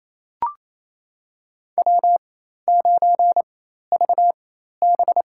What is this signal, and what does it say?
Morse code, heard as a keyed tone, about 27 wpm, spaced as for 13 wpm, 700 Hz. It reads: W9VB
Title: USA Call Signs - 4 Characters - 4x Character Spacing - 28wpm